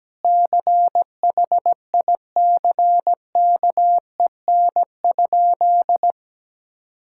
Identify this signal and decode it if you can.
Morse code, heard as a keyed tone, about 17 wpm, 700 Hz. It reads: CHICKEN?
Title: Sentences from Top 1000 Words - Review Part 5 - 17wpm